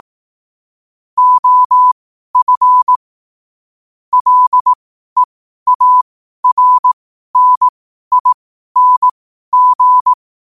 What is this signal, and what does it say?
Morse code, heard as a keyed tone, about 18 wpm, spaced as for 12 wpm, 1000 Hz. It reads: OF LEARNING
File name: qotdfiles / 12wpm